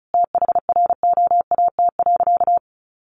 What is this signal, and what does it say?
Morse code, heard as a keyed tone, about 34 wpm, 700 Hz. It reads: THROAT.
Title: A Sound Of Thunder 35wpm